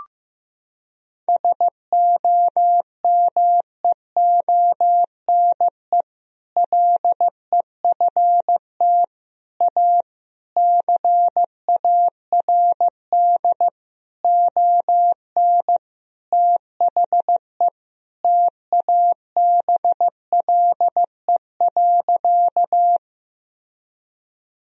Morse code, heard as a keyed tone, about 15 wpm, 700 Hz. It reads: SOMEONE LEFT A CARD ON THE TABLE.
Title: Sentences from Top 1000 Words - Review Part 4 - 15wpm